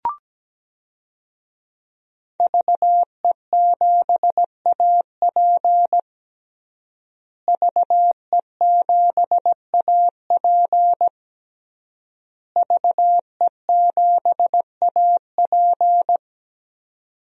Morse code, heard as a keyed tone, about 17 wpm, 700 Hz. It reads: VE7AP VE7AP VE7AP
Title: DXCC-100 Call Signs - 5 Characters - Repeated 3x - 17wpm